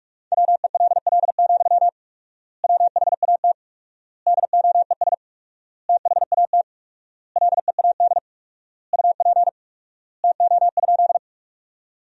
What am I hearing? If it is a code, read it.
Morse code, heard as a keyed tone, about 42 wpm, 700 Hz. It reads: WELL, WHAT DOES THAT LEAD UP TO?
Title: 4x Word-Spacing - Sentences from Top 500 Words - 45wpm